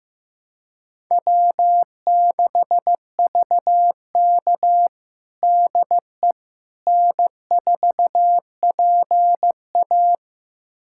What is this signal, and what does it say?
Morse code, heard as a keyed tone, about 16 wpm, 700 Hz. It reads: W6VK DE N4PA
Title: USA Callsign DE Callsign - 4 Characters - 15wpm